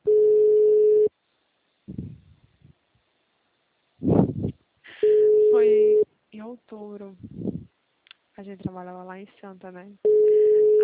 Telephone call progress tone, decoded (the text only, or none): ringback tone